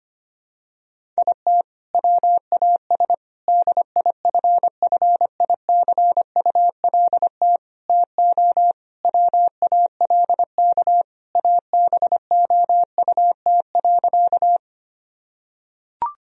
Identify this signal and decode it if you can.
Morse code, heard as a keyed tone, about 25 wpm, 700 Hz. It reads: IT WAS DIFFICULT TO WALK ABOUT.
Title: Sentences from Top 1000 Words - Review Part 1 - 25wpm